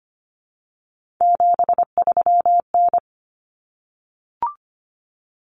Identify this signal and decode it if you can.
Morse code, heard as a keyed tone, about 25 wpm, 700 Hz. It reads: 73N
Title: Common Cut Numbers - 3-Digit 25wpm